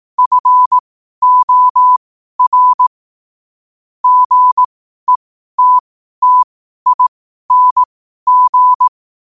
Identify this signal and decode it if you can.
Morse code, heard as a keyed tone, about 18 wpm, spaced as for 12 wpm, 1000 Hz. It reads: FOR GETTING
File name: qotdfiles / 12wpm